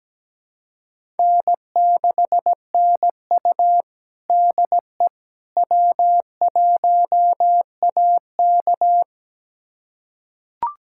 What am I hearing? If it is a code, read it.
Morse code, heard as a keyed tone, about 17 wpm, 700 Hz. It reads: N6NU DE W1AK
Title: USA Callsign DE Callsign - 4 Characters - 17wpm